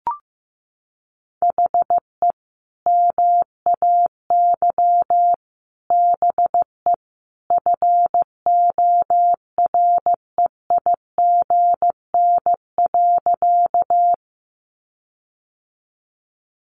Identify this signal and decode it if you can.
Morse code, heard as a keyed tone, about 15 wpm, 700 Hz. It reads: HE MAY BE FOREIGN.
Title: Sentences from Top 1400 Words 15wpm